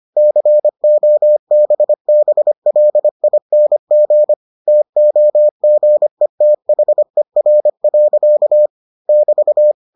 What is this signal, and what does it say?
Morse code, heard as a keyed tone, about 25 wpm, 600 Hz. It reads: COBBLING TOGETHER. <BT>